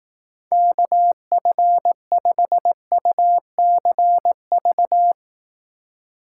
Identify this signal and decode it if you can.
Morse code, heard as a keyed tone, about 18 wpm, 700 Hz. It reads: KF5UCV
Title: Speed Racing - USA 2x3 Call Signs 15wpm